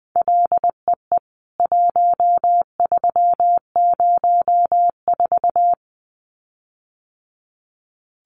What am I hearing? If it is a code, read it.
Morse code, heard as a keyed tone, about 20 wpm, 700 Hz. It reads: LEE 1304
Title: CWT Exchange 20wpm